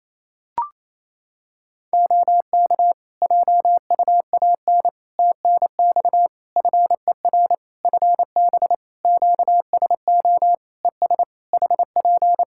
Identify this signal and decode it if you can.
Morse code, heard as a keyed tone, about 28 wpm, 700 Hz. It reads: OK JUAN TNX FER FB QSO ES HP
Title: K1USN SST QSO Practice 28wpm